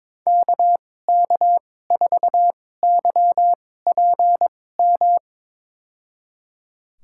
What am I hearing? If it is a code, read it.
Morse code, heard as a keyed tone, about 22 wpm, spaced as for 16 wpm, 700 Hz. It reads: KK4YPM
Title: USA Call Signs - 6 Characters - 2x Character Spacing - 22wpm